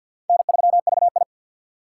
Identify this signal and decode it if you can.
Morse code, heard as a keyed tone, about 48 wpm, 700 Hz. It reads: N3VI
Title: USA Call Signs - 4 Characters - 2x Character Spacing - 50wpm